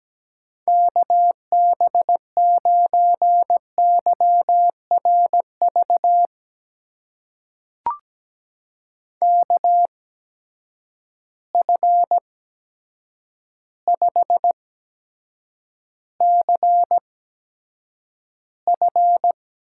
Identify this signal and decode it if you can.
Morse code, heard as a keyed tone, about 17 wpm, spaced as for 8 wpm, 700 Hz. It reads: KB9YRV KF5CF
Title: USA Call Signs - 6 Characters - 8x Character Spacing - 17wpm